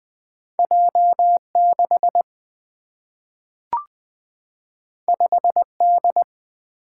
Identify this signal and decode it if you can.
Morse code, heard as a keyed tone, about 20 wpm, 700 Hz. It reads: J 6 5 D
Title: Single-Letter Single-Number 20wpm